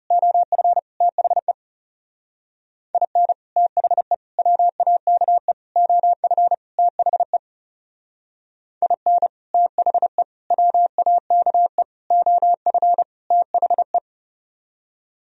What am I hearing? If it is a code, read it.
Morse code, heard as a keyed tone, about 34 wpm, 700 Hz. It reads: OF THE IN THE WAKE OF THE IN THE WAKE OF THE
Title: Speed-Racing - Sets of 5 Words 30wpm